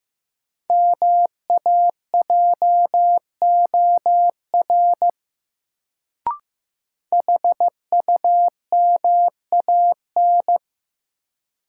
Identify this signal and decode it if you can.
Morse code, heard as a keyed tone, about 15 wpm, 700 Hz. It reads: MAJOR HUMAN
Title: Sentences from Top 600 Words - New Word Drill - 15wpm